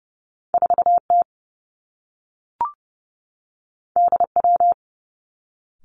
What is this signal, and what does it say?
Morse code, heard as a keyed tone, about 30 wpm, 700 Hz. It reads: 4 T D W
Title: Two Challenging Letter-Number Combinations 30wpm